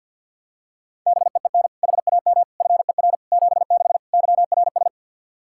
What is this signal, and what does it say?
Morse code, heard as a keyed tone, about 50 wpm, 700 Hz. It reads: BEEN HAM FER 76 YRS
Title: K1USN SST QSO Practice 50wpm